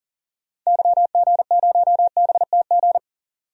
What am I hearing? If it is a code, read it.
Morse code, heard as a keyed tone, about 40 wpm, 700 Hz. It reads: YG0BTG
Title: DXCC-100 Call Signs - 6 Characters - 40wpm